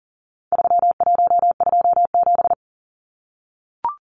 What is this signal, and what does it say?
Morse code, heard as a keyed tone, about 40 wpm, 700 Hz. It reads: 3127
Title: Numbers - 4-Digit 40wpm